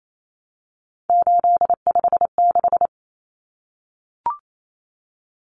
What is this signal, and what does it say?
Morse code, heard as a keyed tone, about 28 wpm, 700 Hz. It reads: 856
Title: Numbers - 3-Digit 28wpm